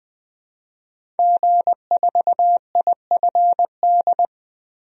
Morse code, heard as a keyed tone, about 20 wpm, 700 Hz. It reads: G4IFD
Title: DXCC-100 Call Signs - 5 Characters - 20wpm